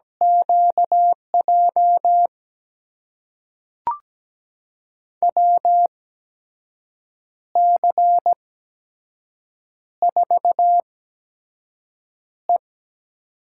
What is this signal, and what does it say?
Morse code, heard as a keyed tone, about 17 wpm, spaced as for 6 wpm, 700 Hz. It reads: QJ WC4E
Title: USA Call Signs - 8x Character Spacing - 17wpm